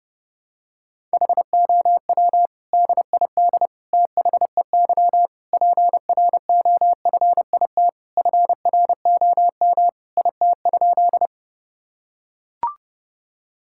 Morse code, heard as a keyed tone, about 30 wpm, 700 Hz. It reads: HOW DID THEY PROFIT FROM IT?
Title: Sentences from Top 1000 Words - Review Part 5 - 30wpm